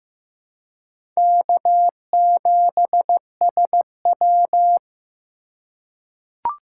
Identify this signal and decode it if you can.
Morse code, heard as a keyed tone, about 15 wpm, 700 Hz. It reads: K7SW
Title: USA Call Signs - 4 Characters - 3x Character Spacing - 15wpm